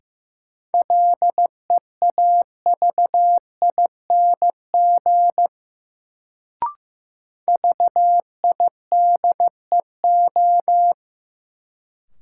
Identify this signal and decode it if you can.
Morse code, heard as a keyed tone, about 15 wpm, 700 Hz. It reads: LEAVING VIDEO
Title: Sentences from Top 1000 Words - Review Part 4 - New Word Drill - 15wpm